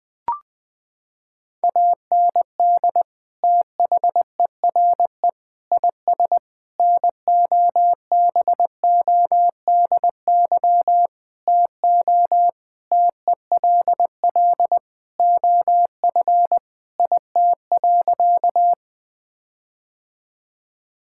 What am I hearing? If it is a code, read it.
Morse code, heard as a keyed tone, about 20 wpm, 700 Hz. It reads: AND THERE IS NOBODY TO TELL OF IT.
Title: Sentences from Top 1600 Words 20wpm